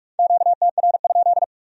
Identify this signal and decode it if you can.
Morse code, heard as a keyed tone, about 57 wpm, 700 Hz. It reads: QTR?